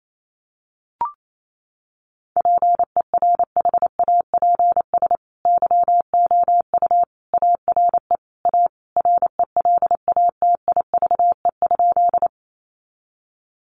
Morse code, heard as a keyed tone, about 28 wpm, 700 Hz. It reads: PERHAPS YOU ARE A RELATIVE?
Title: Sentences from Top 1400 Words 28wpm